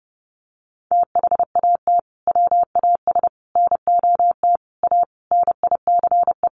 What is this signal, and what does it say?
Morse code, heard as a keyed tone, about 30 wpm, 700 Hz. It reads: THAT WAS NOT A NICE
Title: Sentences from Top 1200 Words 30wpm